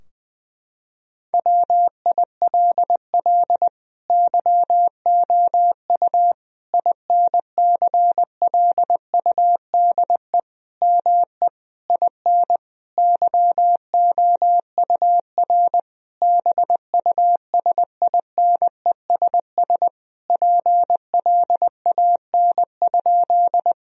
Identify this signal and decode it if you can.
Morse code, heard as a keyed tone, about 20 wpm, 700 Hz. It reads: WILL YOU INCLUDE ME IN YOUR BUSINESS PLAN?
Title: Sentences from Top 1000 Words - Review Part 3 - 20wpm